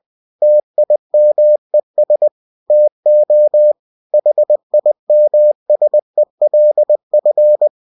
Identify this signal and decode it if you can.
Morse code, heard as a keyed tone, about 20 wpm, 600 Hz. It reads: TIMES TO HIMSELF
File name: cw-books / winnie-the-pooh / winnie-20wpm-page-0081